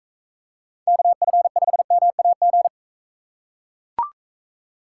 Morse code, heard as a keyed tone, about 41 wpm, 700 Hz. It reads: KF5MAG